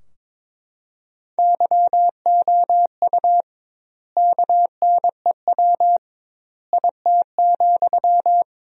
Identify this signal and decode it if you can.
Morse code, heard as a keyed tone, about 21 wpm, 700 Hz. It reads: YOU KNEW IT,
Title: 2x Word-Spacing - Sentences from Top 500 Words - 22wpm